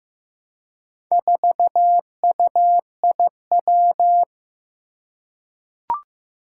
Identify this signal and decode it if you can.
Morse code, heard as a keyed tone, about 15 wpm, 700 Hz. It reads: 4UIW
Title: Four Challenging Letter-Number Combinations 15wpm